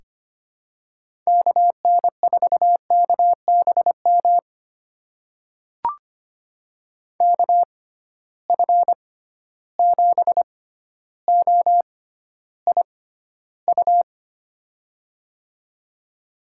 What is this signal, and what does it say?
Morse code, heard as a keyed tone, about 25 wpm, spaced as for 13 wpm, 700 Hz. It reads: KN4KBM KF7OIU
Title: USA Call Signs - 6 Characters - 6x Character Spacing - 25wpm